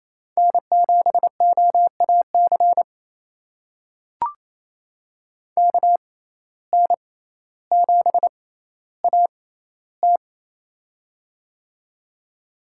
Morse code, heard as a keyed tone, about 28 wpm, spaced as for 14 wpm, 700 Hz. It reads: N7OAC KN7AT
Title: USA Call Signs - 5 Characters - 6x Character Spacing - 28wpm